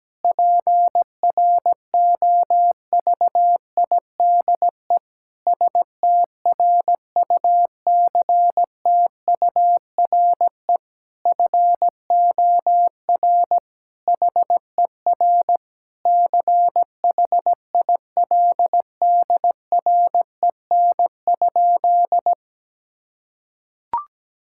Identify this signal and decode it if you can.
Morse code, heard as a keyed tone, about 17 wpm, 700 Hz. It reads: PROVIDE STRUCTURE FOR HER CHILDREN?
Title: Sentences from Top 1000 Words - Review Part 6 - 17wpm